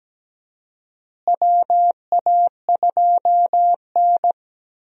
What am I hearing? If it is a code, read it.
Morse code, heard as a keyed tone, about 17 wpm, 700 Hz. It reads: WA2N
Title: USA Call Signs - 4 Characters - Repeated 2x - 17wpm